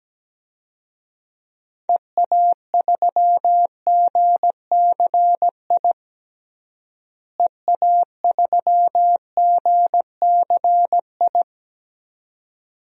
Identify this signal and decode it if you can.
Morse code, heard as a keyed tone, about 17 wpm, 700 Hz. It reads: EA3GCI EA3GCI
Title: DXCC-100 Call Signs - 6 Characters - Repeated 2x - 17wpm